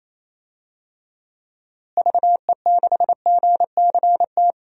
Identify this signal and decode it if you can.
Morse code, heard as a keyed tone, about 28 wpm, 700 Hz. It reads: VE6GCT
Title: DXCC-100 Call Signs - 6 Characters - Repeated 2x - 28wpm